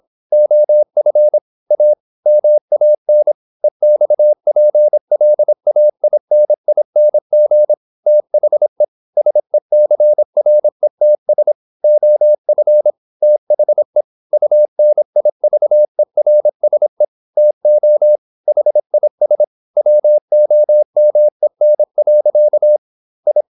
Morse code, heard as a keyed tone, about 26 wpm, 600 Hz. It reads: OF A MAN EXPLAINING THE SECRETS OF THE UNIVERSE TO HIS WOMEN. I
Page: Never 26wpm | CW med Gnister